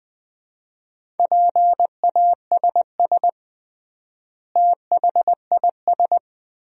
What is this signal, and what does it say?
Morse code, heard as a keyed tone, about 21 wpm, 700 Hz. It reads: PASS THIS
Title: 3x Word-Spacing - Sentences from Top 400 Words - 20wpm